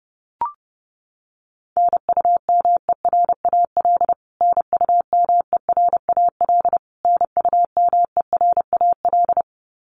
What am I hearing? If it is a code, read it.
Morse code, heard as a keyed tone, about 30 wpm, 700 Hz. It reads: NUMERAL NUMERAL NUMERAL
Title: Top 500 Words Repeated 30wpm